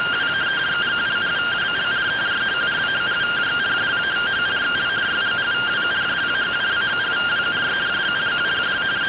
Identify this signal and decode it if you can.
Two-tone teleprinter signal, about 170 Hz shift.